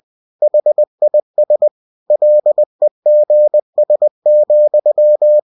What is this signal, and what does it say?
Morse code, heard as a keyed tone, about 20 wpm, 600 Hz. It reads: HIS LEGS,
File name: cw-books / winnie-the-pooh / winnie-20wpm-page-0094